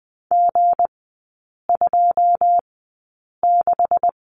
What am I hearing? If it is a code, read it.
Morse code, heard as a keyed tone, about 20 wpm, spaced as for 8 wpm, 700 Hz. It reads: G26
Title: 159 Lesson 26e - Letter g - Three Characters 20wpm